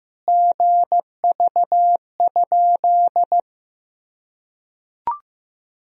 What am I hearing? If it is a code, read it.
Morse code, heard as a keyed tone, about 15 wpm, 700 Hz. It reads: GV?
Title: Sentences from Top 200 Words and Abbreviations 15wpm